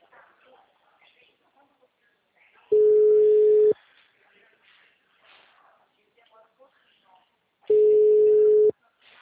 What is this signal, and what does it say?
Ringback tone (Europe, 425 Hz), with speech.